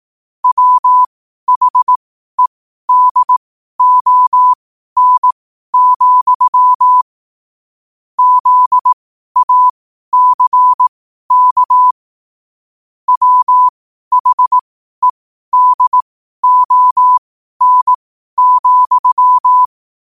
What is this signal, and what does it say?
Morse code, heard as a keyed tone, about 18 wpm, spaced as for 12 wpm, 1000 Hz. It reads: WHEDON, ZACK WHEDON,
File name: qotdfiles / 12wpm